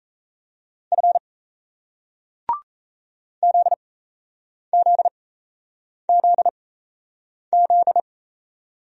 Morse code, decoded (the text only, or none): F ZZZZ